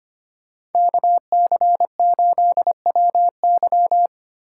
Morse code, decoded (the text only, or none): KC8WY